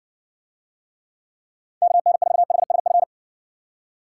Morse code, heard as a keyed tone, about 60 wpm, 700 Hz. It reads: KN4HSF